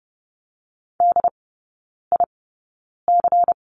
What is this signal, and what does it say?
Morse code, heard as a keyed tone, about 30 wpm, spaced as for 9 wpm, 700 Hz. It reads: DIC
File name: Lesson 18e - Letter m - Three Characters 30wpm